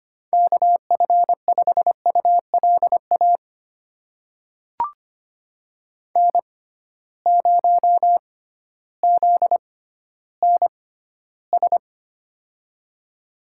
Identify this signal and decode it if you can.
Morse code, heard as a keyed tone, about 25 wpm, spaced as for 14 wpm, 700 Hz. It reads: KF5ULA N0ZNS
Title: USA Call Signs - 6x Character Spacing - 25wpm